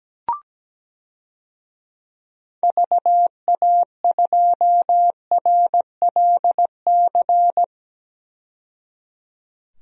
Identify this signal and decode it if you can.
Morse code, heard as a keyed tone, about 17 wpm, 700 Hz. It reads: VA2RLC
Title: DXCC-100 Call Signs - 6 Characters - 17wpm